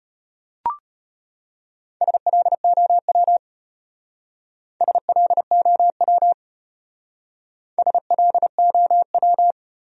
Morse code, heard as a keyed tone, about 31 wpm, 700 Hz. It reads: SLOW SLOW SLOW
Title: Speed-Racing Top 500 Words 25wpm